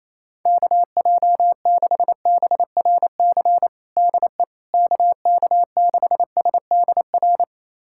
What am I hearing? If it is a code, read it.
Morse code, heard as a keyed tone, about 28 wpm, 700 Hz. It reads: KJ6BRC DE KK6SDR
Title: USA Callsign DE Callsign - 6 Characters - 28wpm